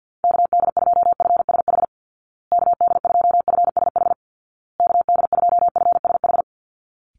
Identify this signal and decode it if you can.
Morse code, heard as a keyed tone, about 49 wpm, 700 Hz. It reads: KD2FSH KD2FSH KD2FSH
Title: Call Signs Repeated 2x - 50wpm